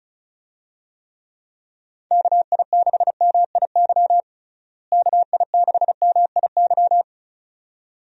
Morse code, heard as a keyed tone, about 36 wpm, 700 Hz. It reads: KI6MIY KI6MIY
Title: USA Call Signs - 6 Characters - Repeated 2x - 35wpm